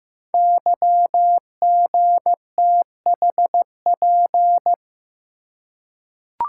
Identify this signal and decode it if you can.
Morse code, heard as a keyed tone, about 15 wpm, 700 Hz. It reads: YGTHP
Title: Code Groups - 5 Letters - 15wpm